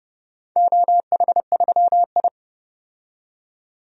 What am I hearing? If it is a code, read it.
Morse code, heard as a keyed tone, about 30 wpm, 700 Hz. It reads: OH3I
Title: DXCC-100 Call Signs - 4 Characters - 30wpm